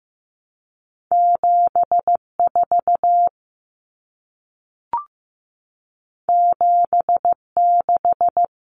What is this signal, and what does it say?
Morse code, heard as a keyed tone, about 15 wpm, 700 Hz. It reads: 7 4 7 6
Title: Common Cut Numbers - 2-Digit 15wpm